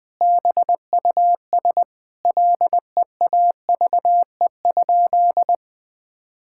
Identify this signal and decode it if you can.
Morse code, heard as a keyed tone, about 20 wpm, 700 Hz. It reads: BUS LEAVE?
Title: Sentences from Top 1000 Words - Review Part 6 - 20wpm